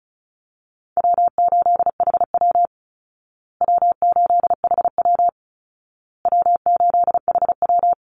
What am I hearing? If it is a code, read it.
Morse code, heard as a keyed tone, about 34 wpm, 700 Hz. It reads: W8HW W8HW W8HW
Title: Easy Call Signs Repeated 3x - 35wpm